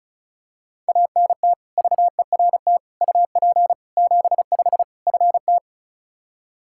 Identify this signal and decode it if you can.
Morse code, heard as a keyed tone, about 34 wpm, 700 Hz. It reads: ANT VERT UP 75 FT
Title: K1USN SST QSO Practice 35wpm